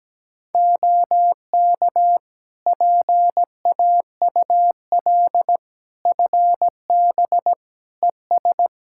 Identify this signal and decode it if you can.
Morse code, heard as a keyed tone, about 17 wpm, 700 Hz. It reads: OK PAUL FB ES
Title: Practice QSO - Exchange 2 - Info - 17wpm